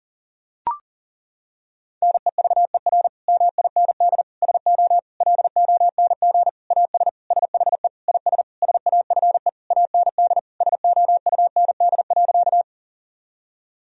Morse code, heard as a keyed tone, about 40 wpm, 700 Hz. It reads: NEVER MIND SO LONG AS SHE IS SAFE AND SOUND.
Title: Sentences from Top 1000 Words - Review Part 1 - 40wpm